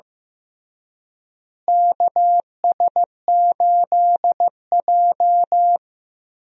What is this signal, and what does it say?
Morse code, heard as a keyed tone, about 16 wpm, 700 Hz. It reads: KS8J
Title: USA Call Signs - 4 Characters - Repeated 3x - 15wpm